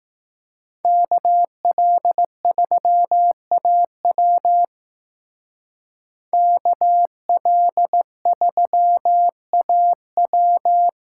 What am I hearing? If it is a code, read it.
Morse code, heard as a keyed tone, about 18 wpm, 700 Hz. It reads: KL3AW KL3AW
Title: Speed Racing - USA 2x2 Call Signs 15wpm